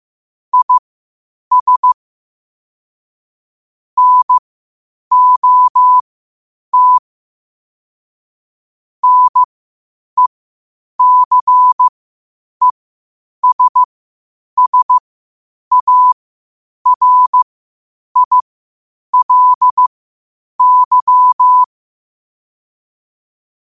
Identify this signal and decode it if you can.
Morse code, heard as a keyed tone, about 15 wpm, spaced as for 8 wpm, 1000 Hz. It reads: IS NOT NECESSARILY
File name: qotdfiles / 8wpm